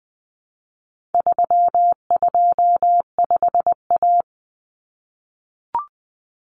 Morse code, heard as a keyed tone, about 20 wpm, 700 Hz. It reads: 325A
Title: Common Cut Numbers - 4-Digit 20wpm